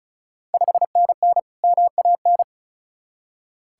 Morse code, heard as a keyed tone, about 36 wpm, 700 Hz. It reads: 5NN MAN
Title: Ontario QSO Party Exchange 35wpm